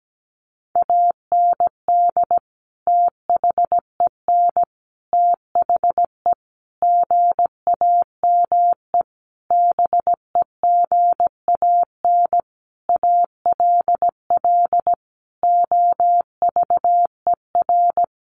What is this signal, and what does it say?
Morse code, heard as a keyed tone, about 17 wpm, 700 Hz. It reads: AND THEN THE GAME BEGAN ALL OVER
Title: Sentences from Top 500 Words 17wpm